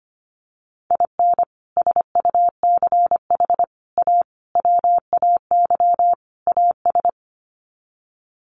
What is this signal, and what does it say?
Morse code, heard as a keyed tone, about 25 wpm, 700 Hz. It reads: IN SUCH A WAY AS
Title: Sets of 5 Words Encore 25wpm